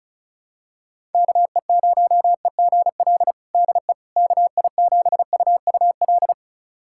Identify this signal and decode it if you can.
Morse code, heard as a keyed tone, about 36 wpm, 700 Hz. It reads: KE0EGL DE KI7UUL
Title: USA Callsign DE Callsign 35wpm